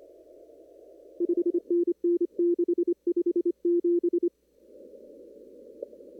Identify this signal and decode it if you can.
Morse code, heard as a keyed tone, about 25 wpm, 350 Hz. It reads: 5NN 657
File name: CW / kt5j wpx cw-op-k5pi